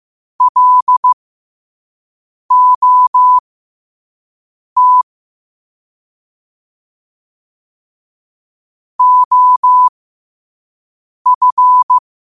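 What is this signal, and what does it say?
Morse code, heard as a keyed tone, about 15 wpm, spaced as for 5 wpm, 1000 Hz. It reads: LOT OF
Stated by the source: Morse code sent at 5 words per minute.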